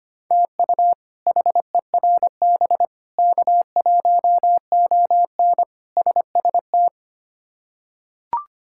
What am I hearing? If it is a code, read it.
Morse code, heard as a keyed tone, about 25 wpm, 700 Hz. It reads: TU HERB K1ON SST
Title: K1USN SST QSO Practice - 25wpm